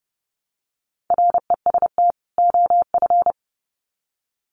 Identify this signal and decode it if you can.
Morse code, heard as a keyed tone, about 30 wpm, 700 Hz. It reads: REST OF
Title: Sets of 2 Words 30wpm